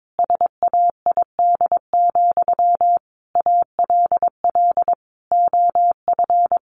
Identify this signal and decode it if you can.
Morse code, heard as a keyed tone, about 22 wpm, 700 Hz. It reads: SAID, ALL OF
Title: Sentences from Top 500 Words 22wpm